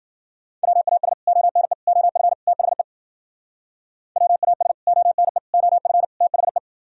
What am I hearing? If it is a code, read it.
Morse code, heard as a keyed tone, about 52 wpm, 700 Hz. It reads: WAS ONE OF THE WAS ONE OF THE